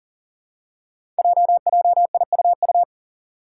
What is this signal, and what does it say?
Morse code, heard as a keyed tone, about 41 wpm, 700 Hz. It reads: JJIUU
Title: Five Challenging Letter-Number Combinations 40wpm